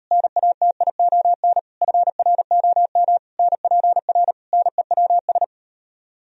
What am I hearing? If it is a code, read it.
Morse code, heard as a keyed tone, about 38 wpm, 700 Hz. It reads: NATION FROM NPR NEWS